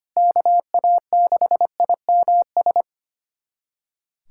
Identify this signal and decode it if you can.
Morse code, heard as a keyed tone, about 25 wpm, 700 Hz. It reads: KA6IMS
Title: USA Callsign DE Callsign - 6 Characters - 25wpm